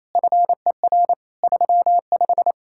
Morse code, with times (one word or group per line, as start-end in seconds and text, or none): 0.2-1.1 FER
1.4-2.5 35